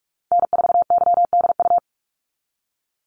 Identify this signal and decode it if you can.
Morse code, heard as a keyed tone, about 46 wpm, 700 Hz. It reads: N4YDU
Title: Call Signs - 45wpm